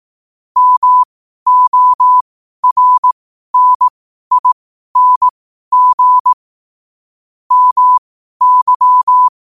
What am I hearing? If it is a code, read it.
Morse code, heard as a keyed tone, about 18 wpm, spaced as for 12 wpm, 1000 Hz. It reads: MORNING MY
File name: qotdfiles / 12wpm